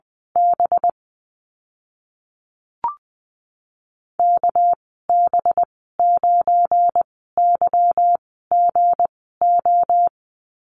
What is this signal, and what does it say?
Morse code, heard as a keyed tone, about 20 wpm, spaced as for 15 wpm, 700 Hz. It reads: B KB9YGO
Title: 200 Lesson 32g - Letter k - Call Signs 20wpm